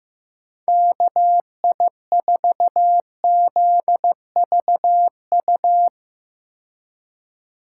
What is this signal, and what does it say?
Morse code, heard as a keyed tone, about 15 wpm, 700 Hz. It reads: KI4ZVU
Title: K1USN SST QSO Practice - 15wpm